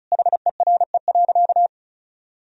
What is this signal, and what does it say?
Morse code, heard as a keyed tone, about 34 wpm, 700 Hz. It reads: HERE.
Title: Sentences from Top 1000 Words - Review Part 4 - 35wpm